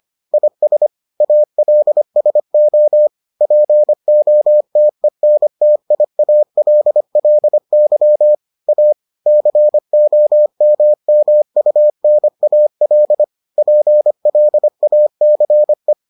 Morse code, heard as a keyed tone, about 25 wpm, 600 Hz. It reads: IS ALSO POTENTIALLY A COMMUNAL PLACE